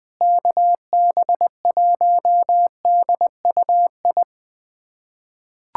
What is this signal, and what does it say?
Morse code, heard as a keyed tone, about 21 wpm, 700 Hz. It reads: KB1DUI
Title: USA Callsign DE Callsign - 6 Characters - 20wpm